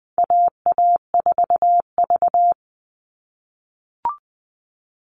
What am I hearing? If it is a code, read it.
Morse code, heard as a keyed tone, about 20 wpm, 700 Hz. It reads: AA4V
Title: All Cut Numbers - 4-Digit 20wpm